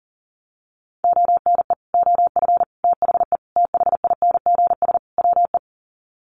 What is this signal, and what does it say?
Morse code, heard as a keyed tone, about 40 wpm, 700 Hz. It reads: ONE OF THE THINGS WE
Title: Sets of 5 Words Encore 40wpm